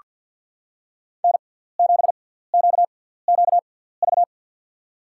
Morse code, heard as a keyed tone, about 47 wpm, spaced as for 17 wpm, 690 Hz. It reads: N7QQV
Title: USA Call Signs - 5 Characters - 6x Character Spacing - 50wpm